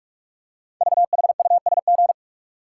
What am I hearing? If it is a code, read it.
Morse code, heard as a keyed tone, about 44 wpm, 690 Hz. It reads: VHUSG
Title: Code Groups - 5 Letters - 45wpm